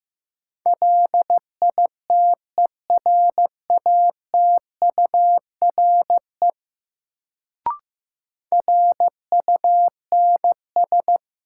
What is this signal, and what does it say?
Morse code, heard as a keyed tone, about 15 wpm, 700 Hz. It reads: LITERATURE RUNS
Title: Sentences from Top 600 Words - New Word Drill - 15wpm